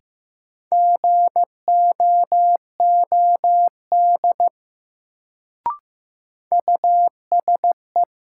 Morse code, heard as a keyed tone, about 15 wpm, 700 Hz. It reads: GOOD USE
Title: Sets of 2 Words - New Word Drill - 15wpm